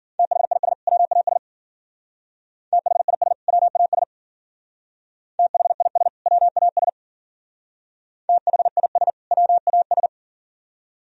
Morse code, heard as a keyed tone, about 49 wpm, 700 Hz. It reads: THIS WAS THIS WAS THIS WAS THIS WAS
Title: Speed-Racing - Sets of 2 Words 40wpm